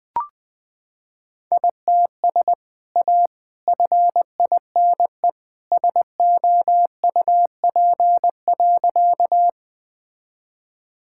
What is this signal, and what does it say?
Morse code, heard as a keyed tone, about 20 wpm, 700 Hz. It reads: ITS A FINE SOUP.
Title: Sentences from Top 1000 Words - Review Part 6 - 20wpm